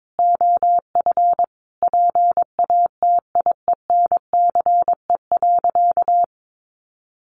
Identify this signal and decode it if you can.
Morse code, heard as a keyed tone, about 22 wpm, 700 Hz. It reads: OF PATIENCE.
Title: Sentences from Top 1300 Words 22wpm